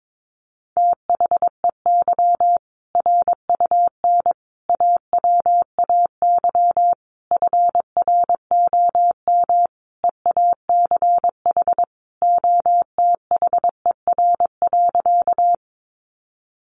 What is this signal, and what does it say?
Morse code, heard as a keyed tone, about 22 wpm, 700 Hz. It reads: THEY RUN AWAY FROM EACH OTHER.
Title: Easy Phrases 22wpm